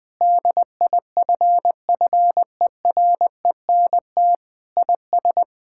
Morse code, heard as a keyed tone, about 20 wpm, 700 Hz. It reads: DIFFERENT IS
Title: Sentences with 8-Word Phrases - Spoken Beginning - 20wpm